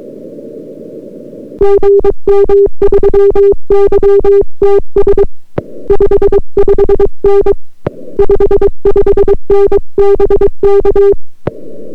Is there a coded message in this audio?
Yes: Morse code from 1.6 to 11.1 s, heard as a keyed tone, about 22 wpm, 400 Hz. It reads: GM3YTS 55N 55N BK